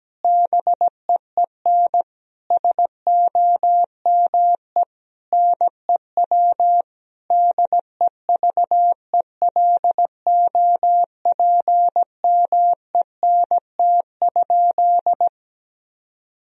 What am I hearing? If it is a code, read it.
Morse code, heard as a keyed tone, about 17 wpm, 700 Hz. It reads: BEEN SOME NEW DEVELOPMENT?
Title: Sentences from Top 1000 Words - Review Part 4 - 17wpm